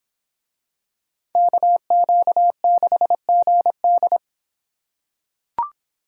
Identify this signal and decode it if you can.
Morse code, heard as a keyed tone, about 26 wpm, 700 Hz. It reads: KQ6GD